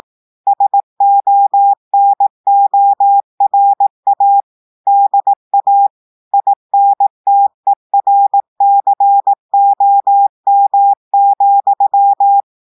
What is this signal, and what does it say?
Morse code, heard as a keyed tone, about 18 wpm, 800 Hz. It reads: SONORA DA INTERCOM,
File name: audios / portugais / portugais texte en clair 1 vitesse 18 mots minutes